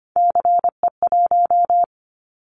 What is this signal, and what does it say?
Morse code, heard as a keyed tone, about 25 wpm, 700 Hz. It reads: CE1
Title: Callsign Prefixes for Top 100 DXCC - 25wpm